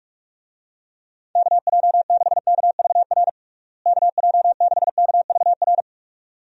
Morse code, heard as a keyed tone, about 46 wpm, 700 Hz. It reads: KJ6KVR KJ6KVR
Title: USA Call Signs - 6 Characters - Repeated 2x - 45wpm